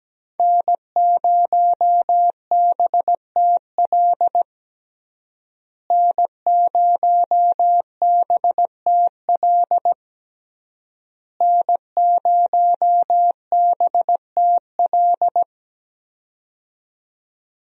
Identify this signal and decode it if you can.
Morse code, heard as a keyed tone, about 17 wpm, 700 Hz. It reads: N0BTL N0BTL N0BTL
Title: USA Call Signs - 5 Characters - Repeated 3x - 17wpm